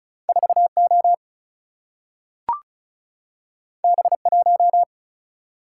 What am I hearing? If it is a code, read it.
Morse code, heard as a keyed tone, about 36 wpm, 700 Hz. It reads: 4 O B 1
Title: Single-Letter Single-Number 35wpm